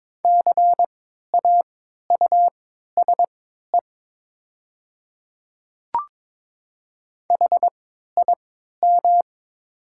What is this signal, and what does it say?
Morse code, heard as a keyed tone, about 22 wpm, spaced as for 13 wpm, 700 Hz. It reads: CAUSE HIM
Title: 3x Character Spacing - Top 200 Words - 22wpm